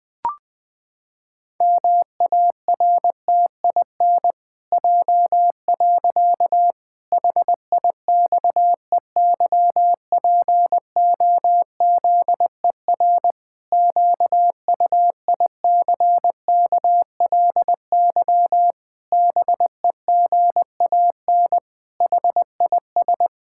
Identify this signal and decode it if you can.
Morse code, heard as a keyed tone, about 20 wpm, 700 Hz. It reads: MARTIN J. HIXEYPOZER QUICKLY BEGAN HIS
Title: Pangram Challenge v2 - 20wpm